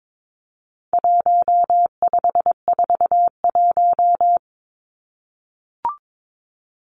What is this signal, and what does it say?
Morse code, heard as a keyed tone, about 22 wpm, 700 Hz. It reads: 1541
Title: Numbers - 4-Digit 22wpm